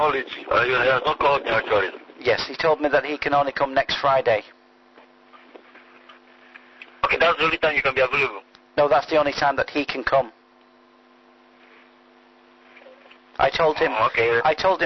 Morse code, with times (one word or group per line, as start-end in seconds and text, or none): none